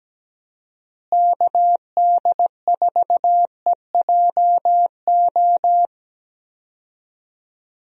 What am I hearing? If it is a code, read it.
Morse code, heard as a keyed tone, about 17 wpm, 700 Hz. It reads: KD4EJO
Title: USA Call Signs - 6 Characters - Repeated 3x - 17wpm